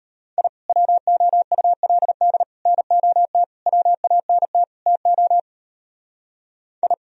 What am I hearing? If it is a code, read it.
Morse code, heard as a keyed tone, about 34 wpm, 700 Hz. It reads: I WOULD NOT WANT TO I
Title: Speed-Racing - Sets of 5 Words 28wpm